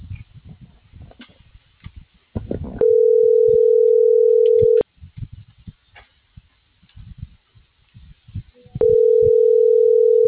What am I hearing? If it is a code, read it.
Ringback tone (North America), 440 + 480 Hz.